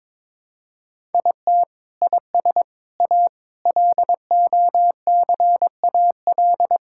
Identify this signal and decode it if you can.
Morse code, heard as a keyed tone, about 22 wpm, 700 Hz. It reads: IT IS A LOCAL
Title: Sentences from Top 1000 Words - Review Part 5 - 22wpm